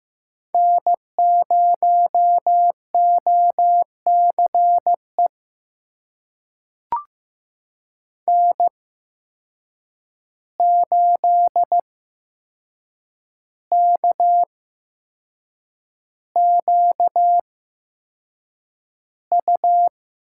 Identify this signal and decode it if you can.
Morse code, heard as a keyed tone, about 15 wpm, spaced as for 6 wpm, 700 Hz. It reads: N0OCE N8KQU
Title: USA Call Signs - 8x Character Spacing - 15wpm